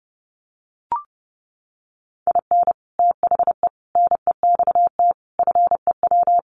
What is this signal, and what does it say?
Morse code, heard as a keyed tone, about 30 wpm, 700 Hz. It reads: IN THE NEXT FEW
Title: Sets of 5 Words Encore 30wpm